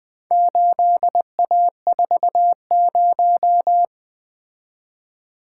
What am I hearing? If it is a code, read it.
Morse code, heard as a keyed tone, about 20 wpm, 700 Hz. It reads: 8A40
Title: Four Letter-Number Combinations - 20wpm